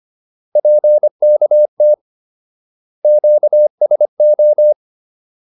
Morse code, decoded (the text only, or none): PKT QSO